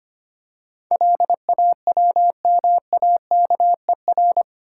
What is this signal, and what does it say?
Morse code, heard as a keyed tone, about 25 wpm, 700 Hz. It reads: LAWMAKER
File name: 8 Letter Words - First 3 Letters Given - 25wpm